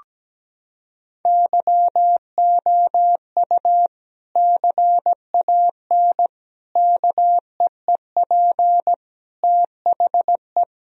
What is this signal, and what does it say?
Morse code, heard as a keyed tone, about 17 wpm, 700 Hz. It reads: YOU CAN KEEP THE
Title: Sentences from Top 1000 Words - Review Part 1 - 17wpm